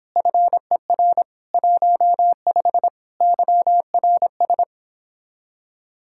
Morse code, heard as a keyed tone, about 29 wpm, 700 Hz. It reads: FER 15 YRS